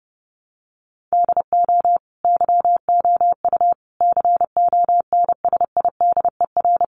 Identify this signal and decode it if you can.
Morse code, heard as a keyed tone, about 30 wpm, 700 Hz. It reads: DO YOU CONSIDER
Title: Sentences from Top 600 Words 30wpm